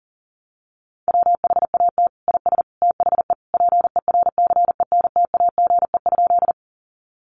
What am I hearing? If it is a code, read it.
Morse code, heard as a keyed tone, about 37 wpm, 700 Hz. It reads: WHAT IS THE PERCENTAGE?
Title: Sentences from Top 1300 Words 40wpm